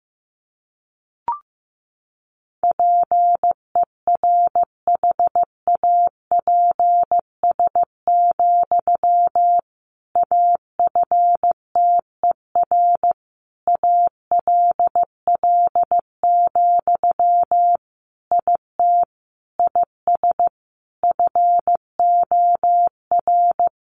Morse code, heard as a keyed tone, about 15 wpm, 700 Hz. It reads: PERHAPS, AFTER ALL, IT IS FOR
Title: Sentences from Top 500 Words 15wpm